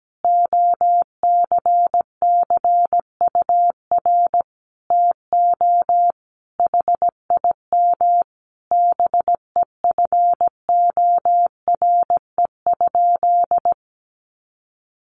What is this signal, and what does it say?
Morse code, heard as a keyed tone, about 17 wpm, 700 Hz. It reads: OCCUR TO HIM BEFORE?
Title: Sentences from Top 900 Words 17wpm